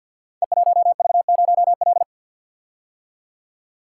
Morse code, heard as a keyed tone, about 46 wpm, 700 Hz. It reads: E1V0L
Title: Code Groups - 5 Letter:Numbers - 50wpm